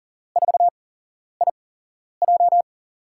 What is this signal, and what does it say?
Morse code, heard as a keyed tone, about 39 wpm, spaced as for 11 wpm, 700 Hz. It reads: 4IJ
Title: USA Call Signs - 5 Characters - 8x Character Spacing - 40wpm